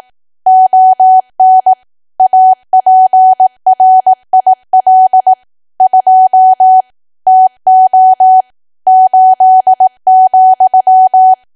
Morse code, heard as a keyed tone, about 18 wpm, 750 Hz. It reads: ON APRIL 2 TO 8,
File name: cwa / W1AW-Level2 / 18WPM